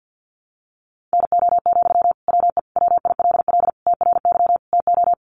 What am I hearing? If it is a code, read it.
Morse code, heard as a keyed tone, about 50 wpm, 700 Hz. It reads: NO, WE WILL TRY TO
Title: Sentences from Top 1100 Words 50wpm